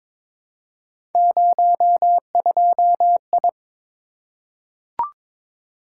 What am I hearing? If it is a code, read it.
Morse code, heard as a keyed tone, about 22 wpm, 700 Hz. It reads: 02I
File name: practice / 22wpm / Three Letter-Number Combinations - 22wpm